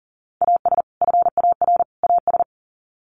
Morse code, heard as a keyed tone, about 40 wpm, 700 Hz. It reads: AS FAR AS
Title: Sets of 3 Words 40wpm